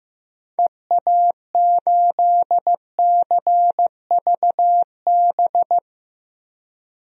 Morse code, heard as a keyed tone, about 15 wpm, 700 Hz. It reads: EA8CVB
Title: DXCC-100 Call Signs - 6 Characters - 15wpm